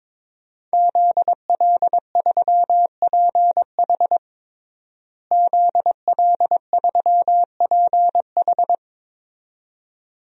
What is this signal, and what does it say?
Morse code, heard as a keyed tone, about 22 wpm, 700 Hz. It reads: ZL3PH ZL3PH
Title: DXCC-100 Call Signs - 5 Characters - Repeated 2x - 22wpm